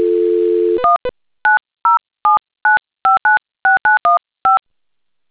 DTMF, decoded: 19079596915